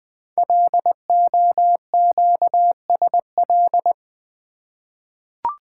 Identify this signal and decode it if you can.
Morse code, heard as a keyed tone, about 20 wpm, 700 Hz. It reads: LOQSL
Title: Code Groups - 5 Letters - 20wpm